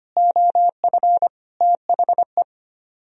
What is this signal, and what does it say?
Morse code, heard as a keyed tone, about 33 wpm, 700 Hz. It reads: OF THE